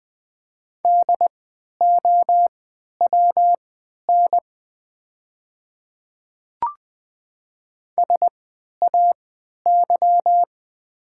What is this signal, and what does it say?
Morse code, heard as a keyed tone, about 20 wpm, spaced as for 11 wpm, 700 Hz. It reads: DOWN SAY
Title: 3x Character Spacing - Top 300 Words - 20wpm